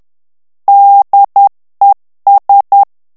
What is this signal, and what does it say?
Morse code, heard as a keyed tone, about 10 wpm, 800 Hz. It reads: DES
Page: Français - texte en clair 1 - vitesse : 10 mots minute